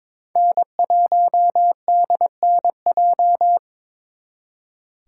Morse code, heard as a keyed tone, about 28 wpm, 700 Hz. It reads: N1DNJ